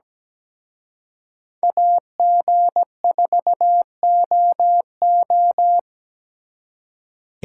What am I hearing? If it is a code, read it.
Morse code, heard as a keyed tone, about 20 wpm, 700 Hz. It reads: AG4OO
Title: Speed Racing - USA 2x2 Call Signs 17wpm